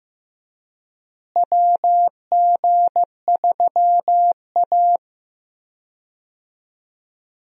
Morse code, heard as a keyed tone, about 15 wpm, 700 Hz. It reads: WG3A
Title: USA Call Signs - 4 Characters - Repeated 3x - 15wpm